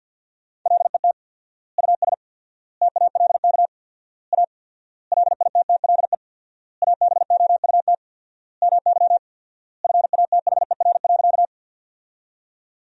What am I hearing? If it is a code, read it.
Morse code, heard as a keyed tone, about 52 wpm, 700 Hz. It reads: LET US TALK A LITTLE ABOUT MY FATHER.
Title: 4x Word-Spacing - Sentences from Top 400 Words - 50wpm